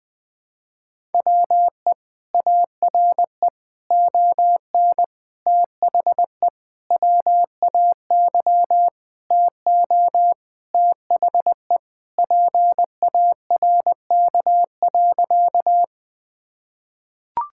Morse code, heard as a keyed tone, about 21 wpm, 700 Hz. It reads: WE ARE ON THE WAY TO THE PARK.
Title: Sentences with 4-Word Phrases - 20wpm